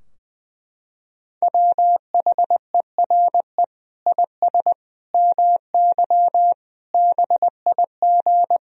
Morse code, heard as a keyed tone, about 20 wpm, 700 Hz. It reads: WHERE IS MY BIG
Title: Sentences from Top 1000 Words - Review Part 2 - 20wpm